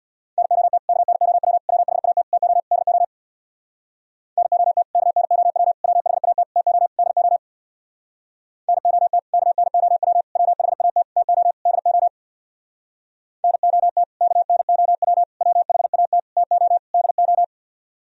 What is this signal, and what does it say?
Morse code, heard as a keyed tone, about 64 wpm, 700 Hz. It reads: NOT KNOW WHAT TO DO NOT KNOW WHAT TO DO NOT KNOW WHAT TO DO NOT KNOW WHAT TO DO